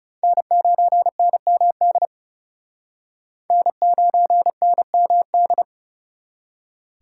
Morse code, heard as a keyed tone, about 37 wpm, 700 Hz. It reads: N9NMD N9NMD
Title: Speed Racing - USA 1x3 Call Signs 30wpm